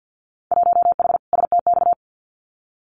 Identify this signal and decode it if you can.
Morse code, heard as a keyed tone, about 50 wpm, 700 Hz. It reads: 1H STX
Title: Winter Field Day Exchanges 50wpm